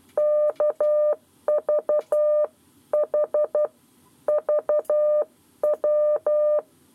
Morse code, heard as a keyed tone, about 12 wpm, 600 Hz. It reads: KVH VW
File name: 7 Lezione cw 1